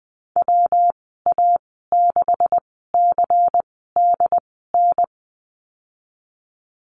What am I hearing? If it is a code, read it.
Morse code, heard as a keyed tone, about 20 wpm, spaced as for 15 wpm, 700 Hz. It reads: WA6CDN
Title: 193 Lesson 31g - Letter v - Call Signs 20wpm